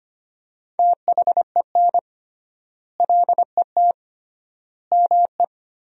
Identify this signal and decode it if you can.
Morse code, heard as a keyed tone, about 25 wpm, 700 Hz. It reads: THEN LET ME
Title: 3x Word-Spacing - Sentences from Top 300 Words - 25wpm